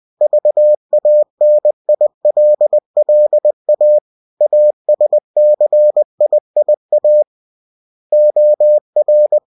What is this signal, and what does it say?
Morse code, heard as a keyed tone, about 20 wpm, 600 Hz. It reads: VANILLA ASCIIA OR
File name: cw-books / winnie-the-pooh / winnie-20wpm-page-0116